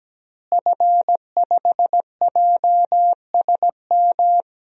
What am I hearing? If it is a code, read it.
Morse code, heard as a keyed tone, about 17 wpm, 700 Hz. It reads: F5JSM
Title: DXCC-100 Call Signs - 5 Characters - Repeated 3x - 17wpm